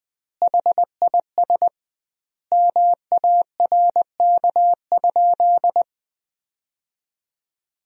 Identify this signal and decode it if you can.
Morse code, heard as a keyed tone, about 21 wpm, 700 Hz. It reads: HIS MARK?
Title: 2x Word-Spacing - Sentences from Top 300 Words - 20wpm